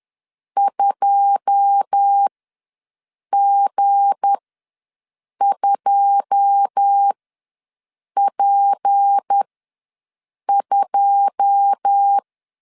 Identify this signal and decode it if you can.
Morse code, heard as a keyed tone, about 10 wpm, spaced as for 6 wpm, 800 Hz. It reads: 2G2P2
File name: LEZIONI / 9 2 ws 10wpm 4130